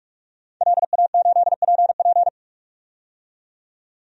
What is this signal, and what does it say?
Morse code, heard as a keyed tone, about 41 wpm, 700 Hz. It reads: LA8PP
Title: DXCC-100 Call Signs - 5 Characters - 45wpm